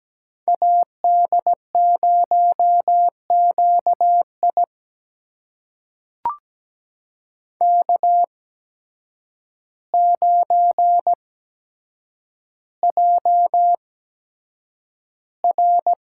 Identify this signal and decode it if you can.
Morse code, heard as a keyed tone, about 17 wpm, spaced as for 8 wpm, 700 Hz. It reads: AD0QI K9JR
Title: USA Call Signs - 8x Character Spacing - 17wpm